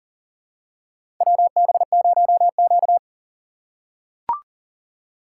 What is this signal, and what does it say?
Morse code, heard as a keyed tone, about 40 wpm, 700 Hz. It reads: WB0Q
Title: USA Call Signs - 4 Characters - 3x Character Spacing - 40wpm